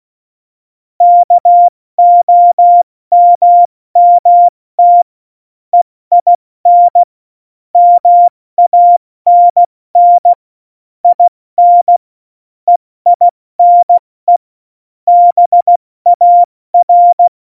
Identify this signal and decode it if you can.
Morse code, heard as a keyed tone, about 16 wpm, 700 Hz. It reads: KOMMT EIN MANN IN EINE BAR
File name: Morsekurs / Witze / Witz 27. November 2018, 70 F80